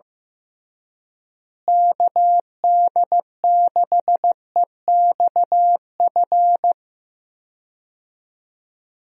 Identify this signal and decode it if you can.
Morse code, heard as a keyed tone, about 15 wpm, 700 Hz. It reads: KD6EXF
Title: USA Call Signs - 6 Characters - Repeated 3x - 15wpm